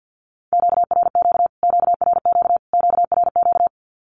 Morse code, heard as a keyed tone, about 50 wpm, 700 Hz. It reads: QRQ QRQ QRQ
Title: Q-Codes Repeated 3x 50wpm